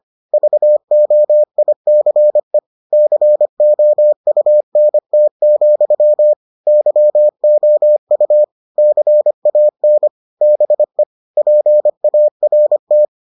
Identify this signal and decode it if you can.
Morse code, heard as a keyed tone, about 25 wpm, 600 Hz. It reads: VOICE COUNT, YOU CAN BE PART